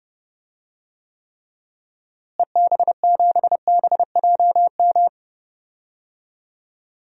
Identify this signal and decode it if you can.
Morse code, heard as a keyed tone, about 30 wpm, 700 Hz. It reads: EB7BJM
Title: DXCC-100 Call Signs - 6 Characters - Repeated 3x - 30wpm